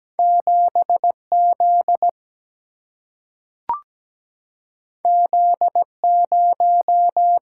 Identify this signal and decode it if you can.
Morse code, heard as a keyed tone, about 17 wpm, 700 Hz. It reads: 7Z Z0
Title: Single-Letter Single-Number 17wpm